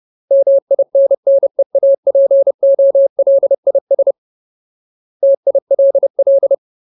Morse code, heard as a keyed tone, about 30 wpm, 550 Hz. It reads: MINNEAPOLIS TILL
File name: Ultimate Word List / 30WPM Ultimate